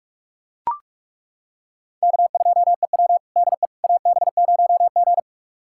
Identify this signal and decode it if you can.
Morse code, heard as a keyed tone, about 42 wpm, 700 Hz. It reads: K2EW DE AB0G
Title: USA Callsign DE Callsign - 4 Characters - 45wpm